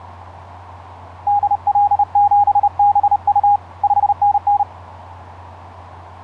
Morse code, heard as a keyed tone, about 30 wpm, 820 Hz. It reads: DL7BU 5NN
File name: 5r8x iz5ebl etc 12m cw 07nov2010 1003